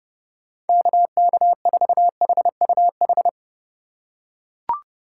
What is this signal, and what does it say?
Morse code, heard as a keyed tone, about 30 wpm, 700 Hz. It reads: KK4HUH
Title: USA Call Signs - 6 Characters - 3x Character Spacing - 30wpm